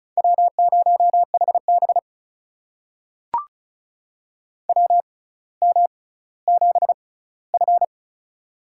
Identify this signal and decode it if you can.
Morse code, heard as a keyed tone, about 36 wpm, spaced as for 18 wpm, 700 Hz. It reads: W0HB WM7F
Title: USA Call Signs - 4 Characters - 6x Character Spacing - 35wpm